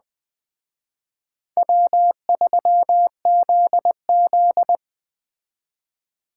Morse code, heard as a keyed tone, about 20 wpm, 700 Hz. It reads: W3ZZ
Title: Speed Racing - USA 1x2 Call Signs 17wpm